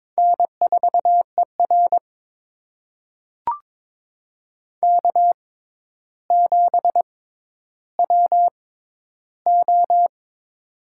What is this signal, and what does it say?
Morse code, heard as a keyed tone, about 22 wpm, spaced as for 11 wpm, 700 Hz. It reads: N4ER K7WO
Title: USA Call Signs - 4 Characters - 6x Character Spacing - 22wpm